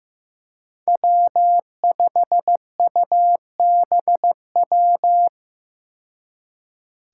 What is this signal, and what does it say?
Morse code, heard as a keyed tone, about 15 wpm, 700 Hz. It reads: W5UBW
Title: Five Challenging Letter-Number Combinations 15wpm